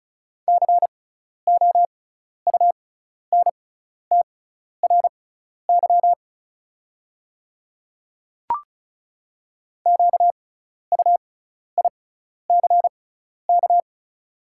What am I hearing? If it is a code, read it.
Morse code, heard as a keyed tone, about 34 wpm, spaced as for 12 wpm, 690 Hz. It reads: COUNTRY QUICK
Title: 6x Character Spacing - Top 500 Words - 35wpm